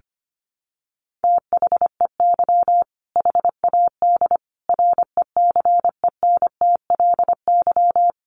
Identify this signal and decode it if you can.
Morse code, heard as a keyed tone, about 25 wpm, 700 Hz. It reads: THEY HAD RECENTLY
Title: Sentences from Top 1200 Words 25wpm